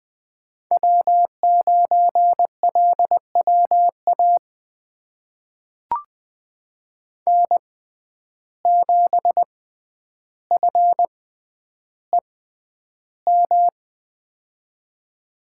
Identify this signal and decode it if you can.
Morse code, heard as a keyed tone, about 20 wpm, spaced as for 10 wpm, 700 Hz. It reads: W9LWA N7FEM
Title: USA Call Signs - 5 Characters - 6x Character Spacing - 20wpm